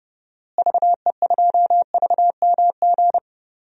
Morse code, heard as a keyed tone, about 32 wpm, 700 Hz. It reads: VE2VMG
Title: DXCC-100 Call Signs - 6 Characters - Repeated 3x - 30wpm